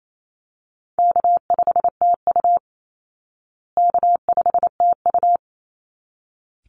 Morse code, heard as a keyed tone, about 28 wpm, 700 Hz. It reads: K5TU K5TU
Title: Call Signs Repeated 3x - 28wpm